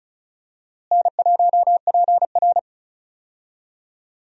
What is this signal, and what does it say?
Morse code, heard as a keyed tone, about 36 wpm, 700 Hz. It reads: N1PR
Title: USA Call Signs - 4 Characters - Repeated 2x - 35wpm